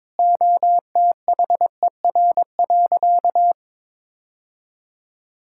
Morse code, heard as a keyed tone, about 22 wpm, 700 Hz. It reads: OTHER.
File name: practice / 22wpm / Sentences with 2-Word Phrases - 22wpm